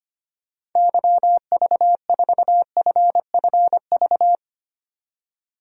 Morse code, heard as a keyed tone, about 25 wpm, 700 Hz. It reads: YV4FFV
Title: DXCC-100 Call Signs - 6 Characters - 25wpm